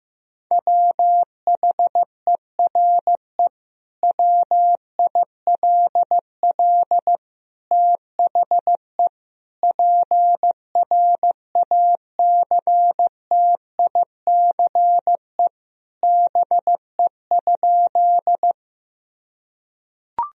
Morse code, heard as a keyed tone, about 16 wpm, 700 Hz. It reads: WHERE WILL THE PRACTICE BE?
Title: Sentences from Top 1000 Words - Review Part 4 - 15wpm